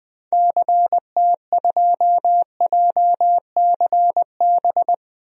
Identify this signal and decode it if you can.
Morse code, heard as a keyed tone, about 21 wpm, 700 Hz. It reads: CT2JCB
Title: DXCC-100 Call Signs - 6 Characters - Repeated 2x - 20wpm